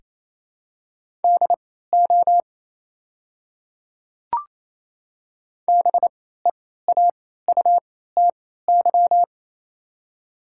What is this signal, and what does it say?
Morse code, heard as a keyed tone, about 28 wpm, spaced as for 16 wpm, 690 Hz. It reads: DO BEAUTY
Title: 3x Character Spacing - Top 500 Words - 28wpm